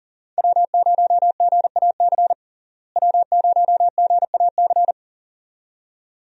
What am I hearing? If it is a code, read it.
Morse code, heard as a keyed tone, about 40 wpm, 700 Hz. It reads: W0GAC W0GAC
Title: USA Call Signs - 5 Characters - Repeated 2x - 40wpm